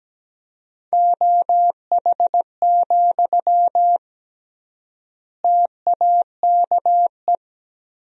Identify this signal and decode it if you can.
Morse code, heard as a keyed tone, about 17 wpm, 700 Hz. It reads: OH, TAKE
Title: 3x Word-Spacing - Sentences from Top 500 Words - 17wpm